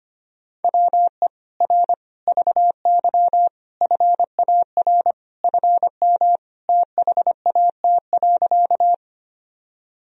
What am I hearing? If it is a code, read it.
Morse code, heard as a keyed tone, about 25 wpm, 700 Hz. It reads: WE R VY FAR FM THAT.
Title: Sentences from Top 300 Words and Abbreviations 25wpm